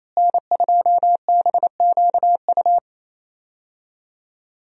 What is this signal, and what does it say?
Morse code, heard as a keyed tone, about 28 wpm, 700 Hz. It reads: N2BQU
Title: USA Call Signs - 5 Characters - Repeated 3x - 28wpm